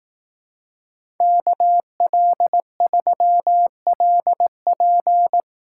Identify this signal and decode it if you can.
Morse code, heard as a keyed tone, about 18 wpm, 700 Hz. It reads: KL3LP
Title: Speed Racing - USA 2x2 Call Signs 15wpm